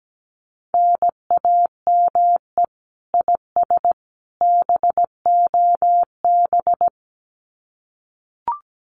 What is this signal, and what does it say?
Morse code, heard as a keyed tone, about 17 wpm, 700 Hz. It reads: NAME IS BOB
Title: Common QSO Phrases 17wpm